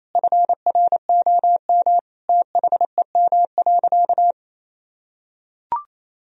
Morse code, heard as a keyed tone, about 28 wpm, 700 Hz. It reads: FROM THEM.
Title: Sentences from Top 1000 Words - Review Part 3 - 28wpm